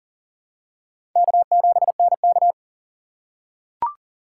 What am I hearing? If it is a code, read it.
Morse code, heard as a keyed tone, about 40 wpm, 700 Hz. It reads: K7NK
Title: USA Call Signs - 4 Characters - 6x Character Spacing - 40wpm